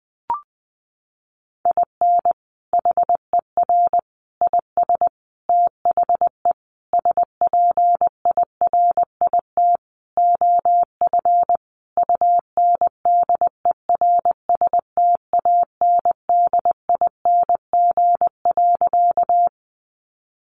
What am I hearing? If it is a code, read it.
Morse code, heard as a keyed tone, about 20 wpm, 700 Hz. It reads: IN HER IS THE SPIRIT OF UNDERSTANDING.
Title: Sentences from Top 1400 Words 20wpm